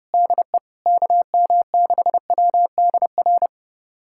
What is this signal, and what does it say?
Morse code, heard as a keyed tone, about 30 wpm, 700 Hz. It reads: DE KM6WDR
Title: USA Callsign DE Callsign - 6 Characters - 30wpm